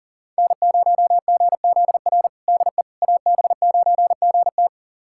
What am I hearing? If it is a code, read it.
Morse code, heard as a keyed tone, about 40 wpm, 700 Hz. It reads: N0GZR DE AB9GT
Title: USA Callsign DE Callsign - 5 Characters - 40wpm